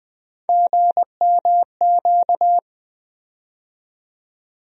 Morse code, heard as a keyed tone, about 21 wpm, 700 Hz. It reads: GMQ
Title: USA Call Sign Postfixes - 3 Letters 20wpm